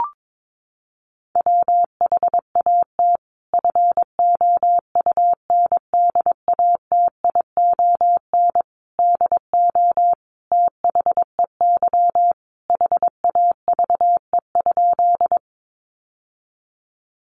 Morse code, heard as a keyed tone, about 22 wpm, 700 Hz. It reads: WHAT FOUNDATION DO THEY HAVE?
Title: Sentences from Top 1200 Words 22wpm